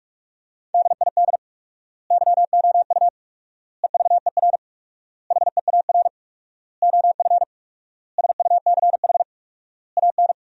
Morse code, heard as a keyed tone, about 41 wpm, 700 Hz. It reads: DID YOU EVER HEAR OF SUCH AN
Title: 4x Word-Spacing - Sentences from Top 400 Words - 45wpm